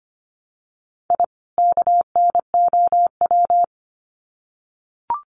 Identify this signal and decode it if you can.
Morse code, heard as a keyed tone, about 25 wpm, 700 Hz. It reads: I KNOW
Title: Sets of 2 Words 25wpm